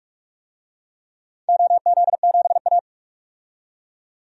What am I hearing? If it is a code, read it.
Morse code, heard as a keyed tone, about 46 wpm, 700 Hz. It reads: OZ7A
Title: DXCC-100 Call Signs - 4 Characters - 45wpm